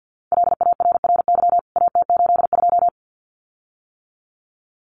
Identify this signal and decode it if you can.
Morse code, heard as a keyed tone, about 50 wpm, 700 Hz. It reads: LARRY AT82
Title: CWT Exchange 50wpm